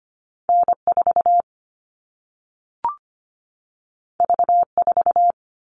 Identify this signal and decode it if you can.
Morse code, heard as a keyed tone, about 25 wpm, 700 Hz. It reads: N4 V4
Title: All Cut Numbers - 2-Digit 25wpm